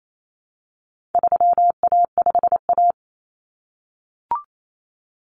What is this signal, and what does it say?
Morse code, heard as a keyed tone, about 28 wpm, 700 Hz. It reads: 3A5A
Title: Common Cut Numbers - 4-Digit 28wpm